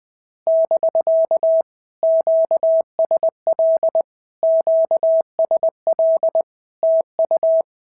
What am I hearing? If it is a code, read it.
Morse code, heard as a keyed tone, about 20 wpm, 650 Hz. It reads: <BK> QSL QSL TU
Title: POTA QSO Practice - 20wpm